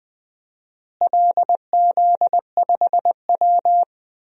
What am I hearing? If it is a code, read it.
Morse code, heard as a keyed tone, about 20 wpm, 700 Hz. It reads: LZ5W
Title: DXCC-100 Call Signs - 4 Characters - Repeated 3x - 20wpm